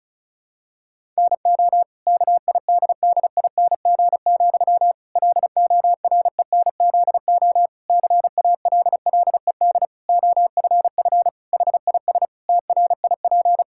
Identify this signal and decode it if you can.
Morse code, heard as a keyed tone, about 34 wpm, 700 Hz. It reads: NO KIDDING, LORENZO CALLED OFF HIS TRIP
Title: Pangram Challenge v2 - 35wpm